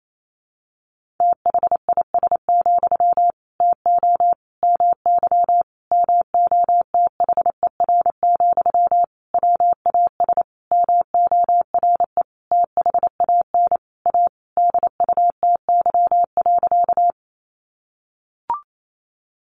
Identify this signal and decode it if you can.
Morse code, heard as a keyed tone, about 28 wpm, 700 Hz. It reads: THIS, TO MY MOTHER, WAS MORE THAN A DUTY.
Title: Sentences from Top 1500 Words 28wpm